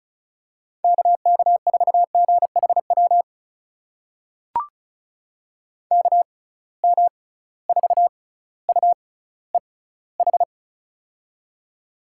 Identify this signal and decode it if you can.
Morse code, heard as a keyed tone, about 36 wpm, spaced as for 18 wpm, 700 Hz. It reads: KK4GHW KM4UEH
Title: USA Call Signs - 6 Characters - 6x Character Spacing - 35wpm